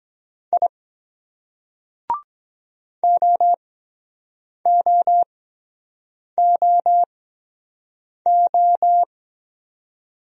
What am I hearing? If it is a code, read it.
Morse code, heard as a keyed tone, about 23 wpm, 700 Hz. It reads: I O O O O